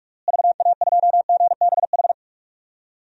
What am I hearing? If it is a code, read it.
Morse code, heard as a keyed tone, about 46 wpm, 700 Hz. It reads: VA2GBH
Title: DXCC-100 Call Signs - 6 Characters - Repeated 3x - 45wpm